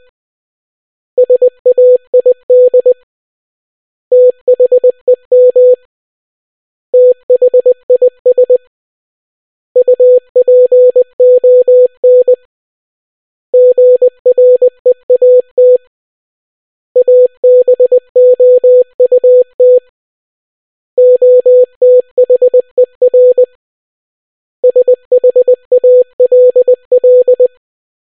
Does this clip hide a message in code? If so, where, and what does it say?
Yes: Morse code from 1.2 to 27.5 s, heard as a keyed tone, about 20 wpm, 500 Hz. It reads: SAID THEM THIS UPON GREAT ABOUT OTHER SHALL